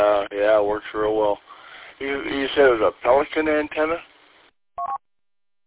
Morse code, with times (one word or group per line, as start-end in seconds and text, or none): none